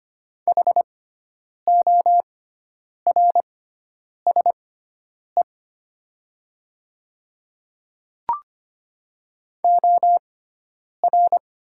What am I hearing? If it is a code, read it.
Morse code, heard as a keyed tone, about 25 wpm, spaced as for 9 wpm, 700 Hz. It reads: HORSE OR
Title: 6x Character Spacing - Top 300 Words - 25wpm